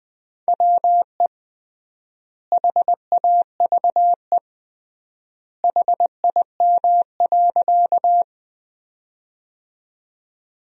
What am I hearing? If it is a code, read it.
Morse code, heard as a keyed tone, about 20 wpm, 700 Hz. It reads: WE HAVE HIM.
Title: 3x Word-Spacing - Sentences from Top 100 Words - 20wpm